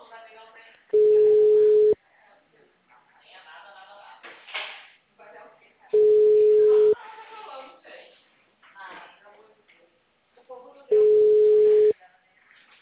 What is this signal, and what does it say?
Ringback tone (Europe, 425 Hz), with speech.
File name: out- -2025- - - .20